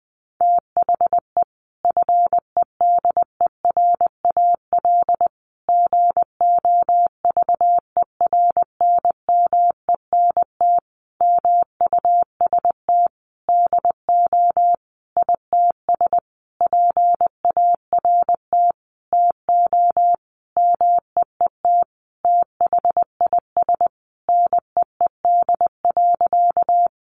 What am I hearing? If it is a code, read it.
Morse code, heard as a keyed tone, about 20 wpm, 700 Hz. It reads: THE FEDERAL GOVERNMENT MUST DO ITS PART TO MEET THIS NEED.
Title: Sentences from Top 1100 Words 20wpm